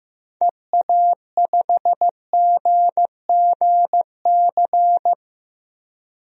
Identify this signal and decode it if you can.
Morse code, heard as a keyed tone, about 16 wpm, 700 Hz. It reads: EA5GGC
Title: DXCC-100 Call Signs - 6 Characters - Repeated 2x - 15wpm